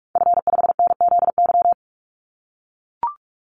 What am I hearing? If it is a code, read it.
Morse code, heard as a keyed tone, about 44 wpm, 700 Hz. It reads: F5NZY
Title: Call Signs - 45wpm